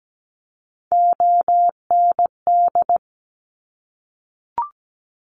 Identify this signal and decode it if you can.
Morse code, heard as a keyed tone, about 17 wpm, 700 Hz. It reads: OND
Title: Top Three Letter Combinations in Words 17wpm